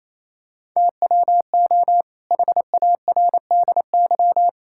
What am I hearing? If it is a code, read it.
Morse code, heard as a keyed tone, about 27 wpm, 700 Hz. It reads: TWO HARDY
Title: Pangram Challenge v2 - 28wpm